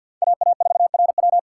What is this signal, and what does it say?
Morse code, heard as a keyed tone, about 62 wpm, 700 Hz. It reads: AA4RW
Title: Speed Racing - USA Call Signs 50wpm